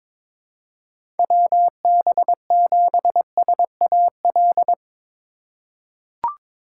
Morse code, heard as a keyed tone, about 22 wpm, 700 Hz. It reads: WB7SAL
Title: USA Call Signs - 6 Characters - 3x Character Spacing - 22wpm